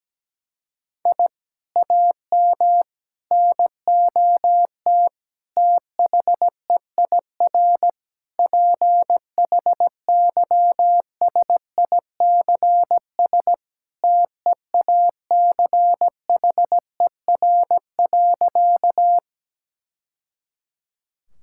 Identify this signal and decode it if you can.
Morse code, heard as a keyed tone, about 17 wpm, 700 Hz. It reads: I AM NOT THEIR PHYSICS TEACHER.
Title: Sentences from Top 1000 Words - Review Part 6 - 17wpm